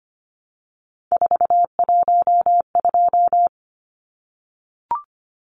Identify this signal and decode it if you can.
Morse code, heard as a keyed tone, about 25 wpm, 700 Hz. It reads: 412
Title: Numbers - 3-Digit 25wpm